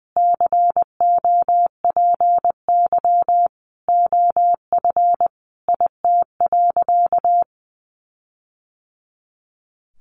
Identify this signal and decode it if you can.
Morse code, heard as a keyed tone, about 20 wpm, 700 Hz. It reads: COPY OF IT.
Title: Sentences from Top 1400 Words 20wpm